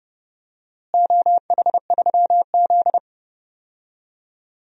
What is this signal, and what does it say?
Morse code, heard as a keyed tone, about 32 wpm, 700 Hz. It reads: OH3Z
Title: DXCC-100 Call Signs - 4 Characters - Repeated 2x - 30wpm